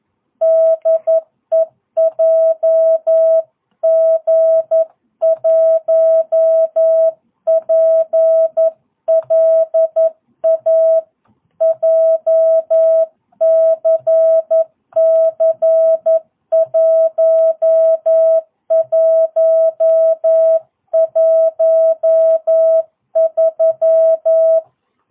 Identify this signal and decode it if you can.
Morse code, heard as a keyed tone, about 11 wpm, 650 Hz. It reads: DE JG1PLA JCC 1113
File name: jh8cbh / 2022cw